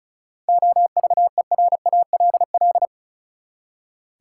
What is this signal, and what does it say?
Morse code, heard as a keyed tone, about 36 wpm, 700 Hz. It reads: OVERALL
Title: Seven Letter Words - 35wpm